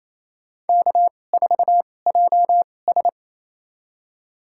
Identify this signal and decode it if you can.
Morse code, heard as a keyed tone, about 28 wpm, spaced as for 20 wpm, 700 Hz. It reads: K4JS
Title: USA Call Signs - 4 Characters - 2x Character Spacing - 28wpm